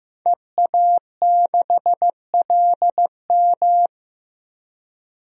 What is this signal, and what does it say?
Morse code, heard as a keyed tone, about 15 wpm, 700 Hz. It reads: EA6LM
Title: DXCC-100 Call Signs - 5 Characters - Repeated 3x - 15wpm